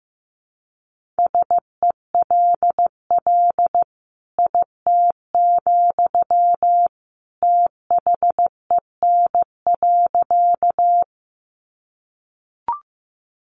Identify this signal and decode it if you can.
Morse code, heard as a keyed tone, about 14 wpm, 700 Hz. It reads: SELL IT, THEN.
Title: Sentences from Top 1100 Words 15wpm